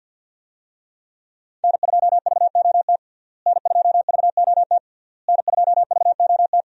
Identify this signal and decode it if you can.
Morse code, heard as a keyed tone, about 52 wpm, 700 Hz. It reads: N2VOT N2VOT N2VOT
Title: USA Call Signs - 5 Characters - Repeated 3x - 50wpm